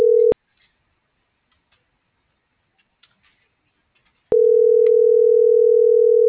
Ringback tone (North America), 440 + 480 Hz.